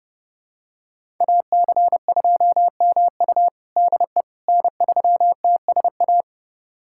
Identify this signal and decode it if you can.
Morse code, heard as a keyed tone, about 30 wpm, 700 Hz. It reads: AC2MU DE N3TSA
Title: USA Callsign DE Callsign - 5 Characters - 30wpm